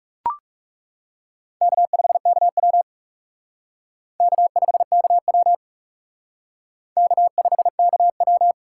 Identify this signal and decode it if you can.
Morse code, heard as a keyed tone, about 38 wpm, 700 Hz. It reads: K5KW K5KW K5KW